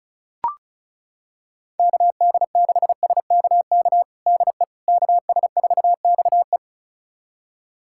Morse code, heard as a keyed tone, about 36 wpm, 700 Hz. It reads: KD6SKK DE KS4XE
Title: USA Callsign DE Callsign 35wpm